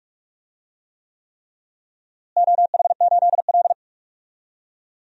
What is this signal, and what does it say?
Morse code, heard as a keyed tone, about 46 wpm, 700 Hz. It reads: OH8L
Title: DXCC-100 Call Signs - 4 Characters - Repeated 3x - 45wpm